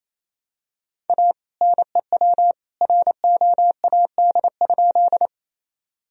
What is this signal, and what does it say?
Morse code, heard as a keyed tone, about 28 wpm, 700 Hz. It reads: A NEW ROAD?
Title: Sentences from Top 1000 Words - Review Part 2 - 28wpm